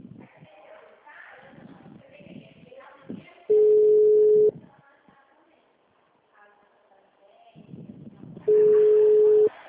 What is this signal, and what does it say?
Ringback tone (Europe, 425 Hz), with speech.
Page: Dracena - SP